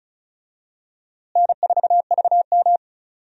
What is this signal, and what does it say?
Morse code, heard as a keyed tone, about 36 wpm, 700 Hz. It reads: N4VM
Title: USA Call Signs - 4 Characters - Repeated 3x - 35wpm